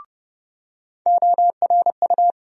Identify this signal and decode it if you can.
Morse code, heard as a keyed tone, about 30 wpm, 700 Hz. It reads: ORU
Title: USA Call Sign Postfixes - 3 Letters 30wpm